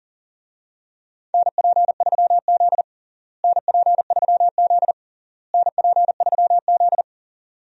Morse code, heard as a keyed tone, about 40 wpm, 700 Hz. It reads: NP3Z NP3Z NP3Z
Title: USA Call Signs - 4 Characters - Repeated 3x - 40wpm